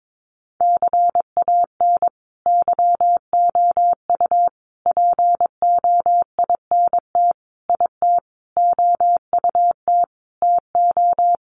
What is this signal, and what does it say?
Morse code, heard as a keyed tone, about 22 wpm, 700 Hz. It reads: CAN YOU POINT IT OUT TO
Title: Sentences from Top 200 Words 22wpm